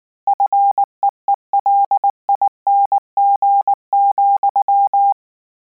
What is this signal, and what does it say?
Morse code, heard as a keyed tone, about 19 wpm, 800 Hz. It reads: FEELING,
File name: Combinations / Combinations-mp3s / Combinations@18